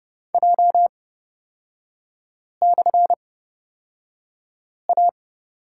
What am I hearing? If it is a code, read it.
Morse code, heard as a keyed tone, about 29 wpm, spaced as for 5 wpm, 690 Hz. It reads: J/A
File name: Lesson 37e - Letter x - Three Characters 30wpm